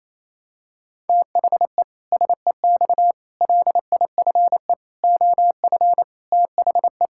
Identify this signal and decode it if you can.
Morse code, heard as a keyed tone, about 28 wpm, 700 Hz. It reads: THE SEX LIFE OF THE
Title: Pangram Challenge v2 - 28wpm